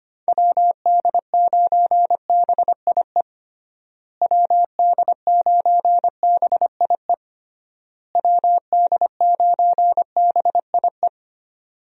Morse code, heard as a keyed tone, about 25 wpm, 700 Hz. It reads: WD9BIE WD9BIE WD9BIE
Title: USA Call Signs - 6 Characters - Repeated 3x - 25wpm